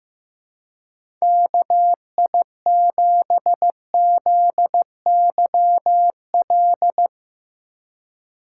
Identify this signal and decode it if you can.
Morse code, heard as a keyed tone, about 15 wpm, 700 Hz. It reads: KI7ZYL
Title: USA Call Signs - 2x Character Spacing - 15wpm